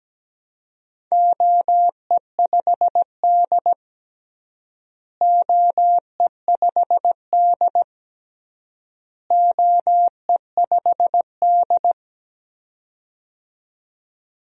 Morse code, heard as a keyed tone, about 17 wpm, 700 Hz. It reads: OE5D OE5D OE5D
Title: DXCC-100 Call Signs - 4 Characters - Repeated 3x - 17wpm